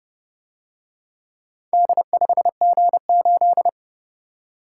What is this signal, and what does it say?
Morse code, heard as a keyed tone, about 30 wpm, 700 Hz. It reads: D5G8